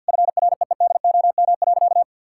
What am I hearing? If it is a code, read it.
Morse code, heard as a keyed tone, about 50 wpm, 700 Hz. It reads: FREEDOM.
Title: Sentences from Top 1000 Words - Review Part 5 - 50wpm